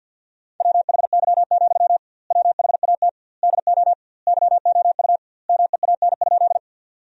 Morse code, heard as a keyed tone, about 46 wpm, 700 Hz. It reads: WHY, WHAT DO YOU MEAN?
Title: 2x Word-Spacing - Sentences from Top 200 Words - 50wpm